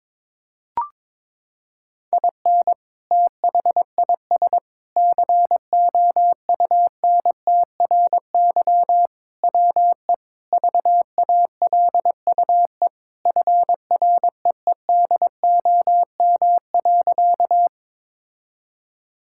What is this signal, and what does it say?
Morse code, heard as a keyed tone, about 22 wpm, 700 Hz. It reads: IN THIS COUNTRY WE VALUE FREEDOM.
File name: practice / 22wpm / Sentences with 3-Word Phrases - 22wpm